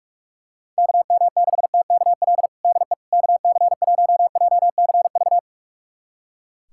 Morse code, heard as a keyed tone, about 42 wpm, 700 Hz. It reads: KM6TKL DE KC1JCV
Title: USA Callsign DE Callsign 45wpm